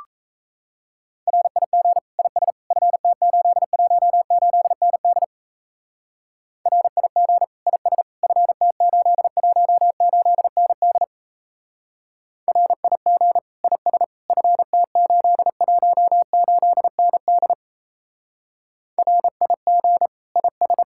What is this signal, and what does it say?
Morse code, heard as a keyed tone, about 35 wpm, 700 Hz. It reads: RIG IS FT818ND RIG IS FT818ND RIG IS FT818ND RIG IS
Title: Speed-Racing Common QSO Phrases - 28wpm